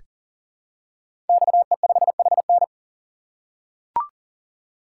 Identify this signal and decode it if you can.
Morse code, heard as a keyed tone, about 39 wpm, 690 Hz. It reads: XE5HN
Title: Code Groups - 5 Letter:Numbers - 40wpm